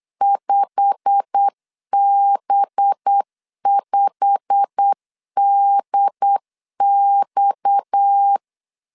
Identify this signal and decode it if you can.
Morse code, heard as a keyed tone, about 8 wpm, 800 Hz. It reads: 5B5DX
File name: CW / DET / 1 9 1500C S B 8wpm 5624